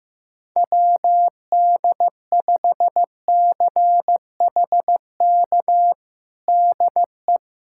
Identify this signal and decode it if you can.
Morse code, heard as a keyed tone, about 15 wpm, 700 Hz. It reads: WD5CHK DE
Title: USA Callsign DE Callsign 15wpm